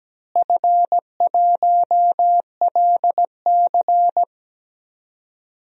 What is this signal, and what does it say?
Morse code, heard as a keyed tone, about 17 wpm, 700 Hz. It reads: F1LC
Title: DXCC-100 Call Signs - 4 Characters - Repeated 3x - 17wpm